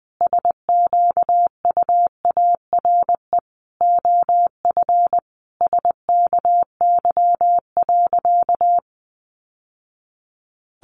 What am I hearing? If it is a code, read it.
Morse code, heard as a keyed tone, about 19 wpm, 700 Hz. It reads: SQUARE OF SKY.
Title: Sentences from Top 1200 Words 20wpm